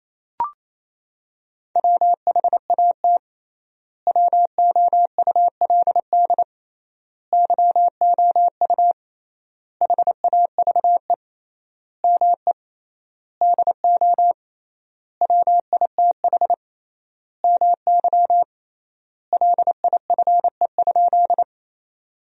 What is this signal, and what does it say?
Morse code, heard as a keyed tone, about 28 wpm, 700 Hz. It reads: WHAT WOULD YOU HAVE ME DO WITH MY LIFE?
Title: 3x Word-Spacing - Sentences from Top 300 Words - 28wpm